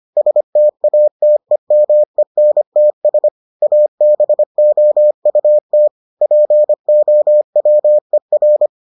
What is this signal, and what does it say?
Morse code, heard as a keyed tone, about 25 wpm, 600 Hz. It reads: STATEMENTS ABOUT POWER